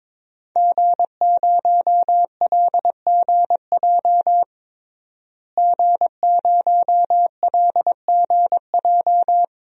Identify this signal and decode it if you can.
Morse code, heard as a keyed tone, about 23 wpm, 700 Hz. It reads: G0LGJ G0LGJ
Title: DXCC-100 Call Signs - 5 Characters - Repeated 2x - 22wpm